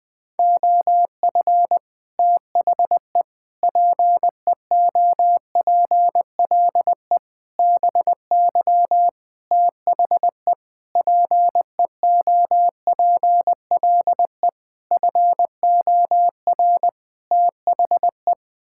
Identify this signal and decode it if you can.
Morse code, heard as a keyed tone, about 20 wpm, 700 Hz. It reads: OF THE PEOPLE BY THE PEOPLE FOR THE
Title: Sets of 8 Words - 20wpm